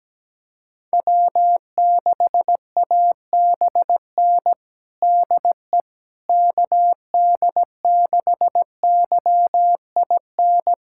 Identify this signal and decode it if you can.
Morse code, heard as a keyed tone, about 17 wpm, 700 Hz. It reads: W6ABN DE KD6YIN
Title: USA Callsign DE Callsign 17wpm